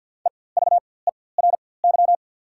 Morse code, heard as a keyed tone, about 48 wpm, spaced as for 23 wpm, 700 Hz. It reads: EVERY
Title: 4x Character Spacing - Top 300 Words - 50wpm